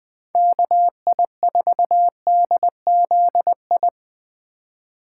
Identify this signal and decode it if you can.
Morse code, heard as a keyed tone, about 20 wpm, 700 Hz. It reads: KI4DZI
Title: USA Call Signs - 6 Characters - 2x Character Spacing - 20wpm